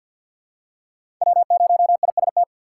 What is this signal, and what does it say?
Morse code, heard as a keyed tone, about 53 wpm, 700 Hz. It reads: W0IST
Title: USA Call Signs - 5 Characters - 4x Character Spacing - 50wpm